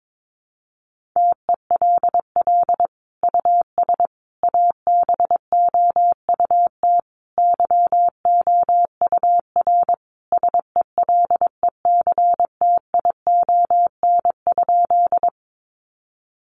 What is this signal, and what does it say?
Morse code, heard as a keyed tone, about 22 wpm, 700 Hz. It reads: TELL US ABOUT YOUR SELECTION?
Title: Sentences from Top 1400 Words 22wpm